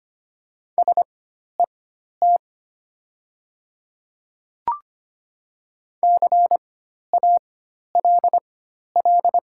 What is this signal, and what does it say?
Morse code, heard as a keyed tone, about 25 wpm, spaced as for 12 wpm, 700 Hz. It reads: SET CALL
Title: 4x Character Spacing - Top 200 Words - 25wpm